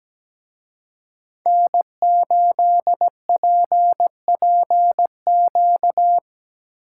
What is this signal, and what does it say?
Morse code, heard as a keyed tone, about 19 wpm, 700 Hz. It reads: N8PPQ
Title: Speed Racing - USA 1x3 Call Signs 17wpm